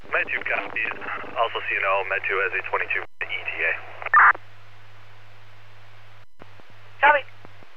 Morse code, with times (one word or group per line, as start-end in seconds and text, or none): none